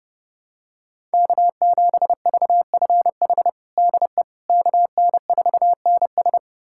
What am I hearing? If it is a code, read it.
Morse code, heard as a keyed tone, about 30 wpm, 700 Hz. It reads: K7VFH DE KN4NS
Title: USA Callsign DE Callsign - 5 Characters - 30wpm